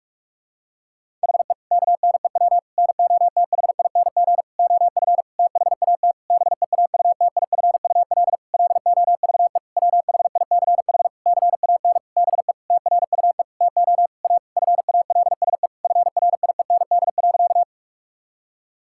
Morse code, heard as a keyed tone, about 46 wpm, 700 Hz. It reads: HE KNEW NOTHING OF THAT BEAUTIFUL LOVE WHICH CAN BE TRUE TO A FALSE FRIEND.
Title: Sentences from Top 1600 Words 45wpm